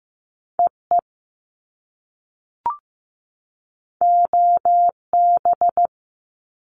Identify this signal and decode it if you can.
Morse code, heard as a keyed tone, about 15 wpm, 700 Hz. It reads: EE OB
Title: Top Two Letter Combinations in Words 15wpm